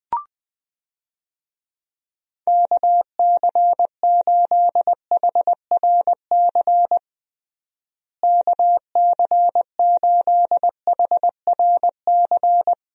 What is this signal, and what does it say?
Morse code, heard as a keyed tone, about 21 wpm, 700 Hz. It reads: KC8HRC KC8HRC
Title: USA Call Signs - 6 Characters - Repeated 3x - 20wpm